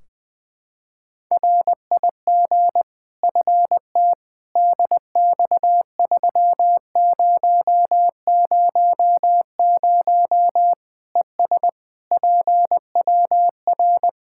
Morse code, heard as a keyed tone, about 20 wpm, 700 Hz. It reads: RIG FT DX3000 ES PWR
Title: Practice QSO - Exchange 2 - Info - 20wpm